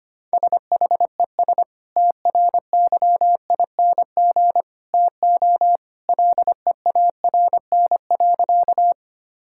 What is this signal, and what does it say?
Morse code, heard as a keyed tone, about 25 wpm, 700 Hz. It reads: SHES TRYING TO LEARN.
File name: practice / 25wpm / Sentences with 2-Word Phrases - 25wpm